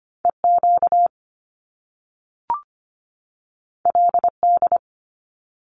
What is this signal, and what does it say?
Morse code, heard as a keyed tone, about 25 wpm, 700 Hz. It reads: EQ LD
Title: Top Two Letter Combinations in Words 25wpm